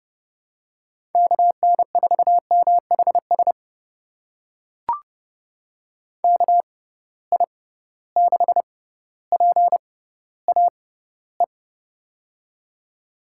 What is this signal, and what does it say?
Morse code, heard as a keyed tone, about 30 wpm, spaced as for 15 wpm, 700 Hz. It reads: KN4MHS KI6PAE
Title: USA Call Signs - 6 Characters - 6x Character Spacing - 30wpm